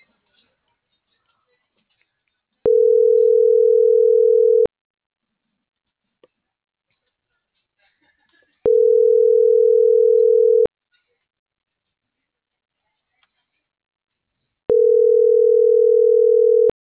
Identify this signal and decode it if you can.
Ringback tone (North America), 440 + 480 Hz.